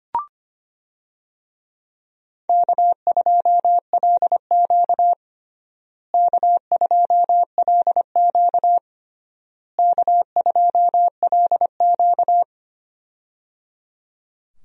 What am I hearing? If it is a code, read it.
Morse code, heard as a keyed tone, about 25 wpm, 700 Hz. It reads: K2LQ K2LQ K2LQ
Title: USA Call Signs - 4 Characters - Repeated 3x - 25wpm